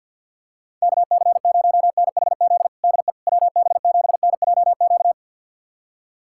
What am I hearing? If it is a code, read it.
Morse code, heard as a keyed tone, about 47 wpm, 700 Hz. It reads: KC0NHZ DE WB7NJQ
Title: USA Callsign DE Callsign - 6 Characters - 50wpm